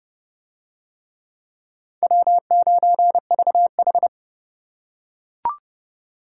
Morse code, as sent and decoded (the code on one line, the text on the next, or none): .-- ----. ...- ....
W9VH